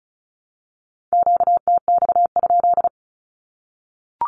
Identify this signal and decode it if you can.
Morse code, heard as a keyed tone, about 34 wpm, 700 Hz. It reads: QTX?
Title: Q-Codes Repeated 3x 35wpm